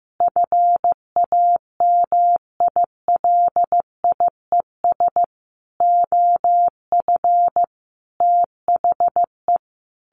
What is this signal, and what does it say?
Morse code, heard as a keyed tone, about 15 wpm, 700 Hz. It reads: FAMILIES OF THE
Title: A Sound Of Thunder 15wpm